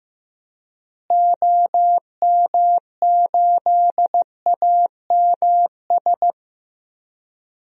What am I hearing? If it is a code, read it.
Morse code, heard as a keyed tone, about 15 wpm, 700 Hz. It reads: OM8AMS
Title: DXCC-100 Call Signs - 6 Characters - Repeated 3x - 15wpm